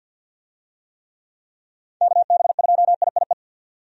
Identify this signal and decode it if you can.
Morse code, heard as a keyed tone, about 51 wpm, 700 Hz. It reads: KB2IIE
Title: USA Call Signs - 6 Characters - 50wpm